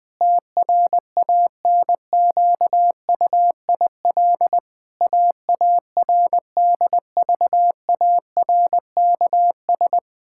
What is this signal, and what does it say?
Morse code, heard as a keyed tone, about 21 wpm, 700 Hz. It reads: TRANQUIL AARDVARKS
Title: Pangram Challenge v2 - 20wpm